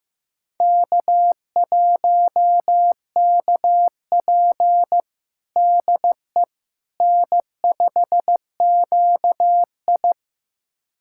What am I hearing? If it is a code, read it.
Morse code, heard as a keyed tone, about 16 wpm, 700 Hz. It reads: K1KP DE N5QI
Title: USA Callsign DE Callsign - 4 Characters - 15wpm